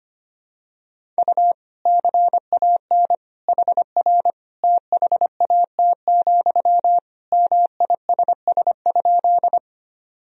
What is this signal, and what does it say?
Morse code, heard as a keyed tone, about 25 wpm, 700 Hz. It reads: U CAN HR THAT, MISS?
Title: Sentences from Top 500 Words and Abbreviations 25wpm